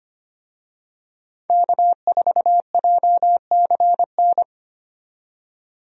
Morse code, heard as a keyed tone, about 25 wpm, 700 Hz. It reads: K4JCN
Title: USA Call Signs - 5 Characters - 25wpm